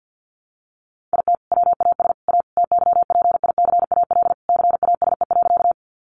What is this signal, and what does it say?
Morse code, heard as a keyed tone, about 49 wpm, 700 Hz. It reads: IT WAS A TYPICAL CASE.
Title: Sentences from Top 1100 Words 50wpm